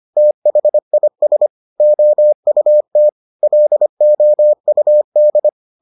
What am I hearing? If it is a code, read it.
Morse code, heard as a keyed tone, about 25 wpm, 600 Hz. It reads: THIS OUT LOUD